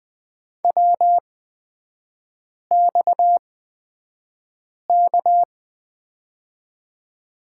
Morse code, heard as a keyed tone, about 20 wpm, spaced as for 5 wpm, 700 Hz. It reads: WXK
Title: 255 Lesson 41e - Period - Three Characters 20wpm